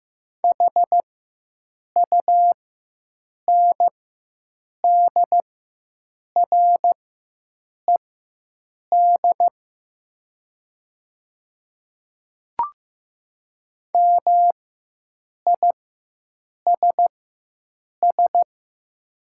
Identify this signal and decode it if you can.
Morse code, heard as a keyed tone, about 15 wpm, spaced as for 7 wpm, 700 Hz. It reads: HUNDRED MISS
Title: 4x Character Spacing - Top 500 Words - 15wpm